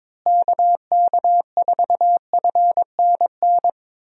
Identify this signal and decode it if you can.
Morse code, heard as a keyed tone, about 22 wpm, 700 Hz. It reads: KK4FNN
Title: USA Call Signs - 6 Characters - Repeated 2x - 22wpm